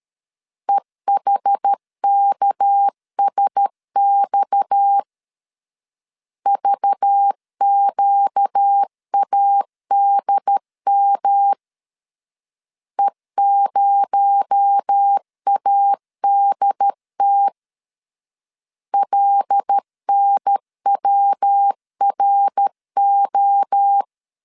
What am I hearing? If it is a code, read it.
Morse code, heard as a keyed tone, about 13 wpm, 800 Hz. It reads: EHKSX VQADM E0ADT LNWRO
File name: CW / DET / 1 6 1500C S B 12wpm 3431